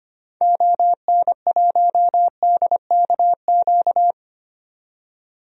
Morse code, heard as a keyed tone, about 25 wpm, 700 Hz. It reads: ON1DKQ
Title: DXCC-100 Call Signs - 6 Characters - 25wpm